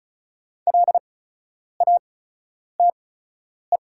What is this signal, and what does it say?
Morse code, heard as a keyed tone, about 34 wpm, spaced as for 10 wpm, 690 Hz. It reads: LATE
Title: 8x Character Spacing - Top 500 Words - 35wpm